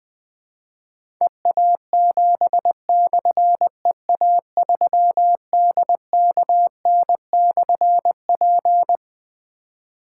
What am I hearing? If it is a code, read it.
Morse code, heard as a keyed tone, about 20 wpm, 700 Hz. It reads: EA7/EA3DKN/P
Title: Difficult Call Signs 20wpm